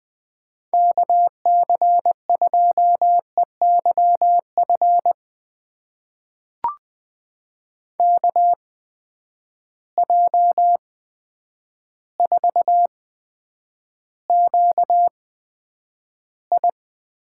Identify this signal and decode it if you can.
Morse code, heard as a keyed tone, about 20 wpm, spaced as for 9 wpm, 700 Hz. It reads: KC2EYF KJ4QI
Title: USA Call Signs - 6 Characters - 8x Character Spacing - 20wpm